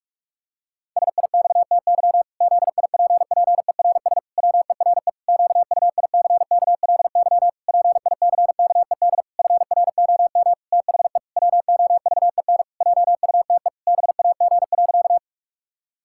Morse code, heard as a keyed tone, about 46 wpm, 700 Hz. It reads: SIXTY ZIPPERS WERE QUICKLY PICKED FROM THE WOVEN JUTE BAG.
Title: Pangram Challenge v2 - 45wpm